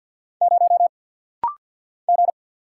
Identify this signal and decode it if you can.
Morse code, heard as a keyed tone, about 50 wpm, 690 Hz. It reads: 0 G
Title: Single Letter-Number - Rapid-Fire 50wpm